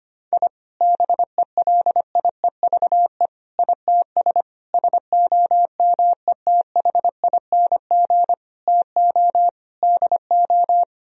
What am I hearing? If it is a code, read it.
Morse code, heard as a keyed tone, about 25 wpm, 700 Hz. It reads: I BELIEVE ITS SOMETHING TO DO
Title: Sentences from Top 1000 Words - Review Part 3 - 25wpm